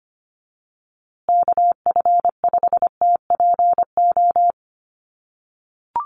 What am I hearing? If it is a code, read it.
Morse code, heard as a keyed tone, about 25 wpm, 700 Hz. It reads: KF5TPO
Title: USA Call Signs Repeated 2x 25wpm